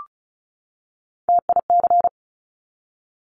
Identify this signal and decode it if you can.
Morse code, heard as a keyed tone, about 36 wpm, 700 Hz. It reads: TIC
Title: Top Three Letter Combinations in Words 35wpm